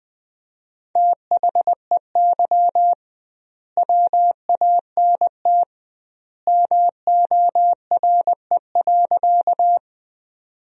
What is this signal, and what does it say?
Morse code, heard as a keyed tone, about 20 wpm, 700 Hz. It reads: THEY WANT MORE.
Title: 2x Word-Spacing - Sentences from Top 200 Words - 20wpm